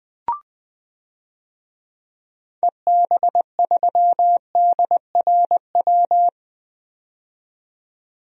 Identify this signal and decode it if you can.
Morse code, heard as a keyed tone, about 20 wpm, 700 Hz. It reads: EB3DRW
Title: DXCC-100 Call Signs - 6 Characters - 20wpm